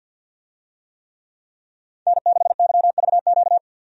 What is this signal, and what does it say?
Morse code, heard as a keyed tone, about 47 wpm, 700 Hz. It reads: N6YVQ
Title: USA Call Signs - 5 Characters - 50wpm